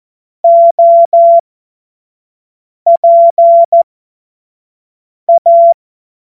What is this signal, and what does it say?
Morse code, heard as a keyed tone, about 14 wpm, spaced as for 5 wpm, 680 Hz. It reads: OPA
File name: Morsekurs / Witze / Witz 25. Mai 2021, 25 F70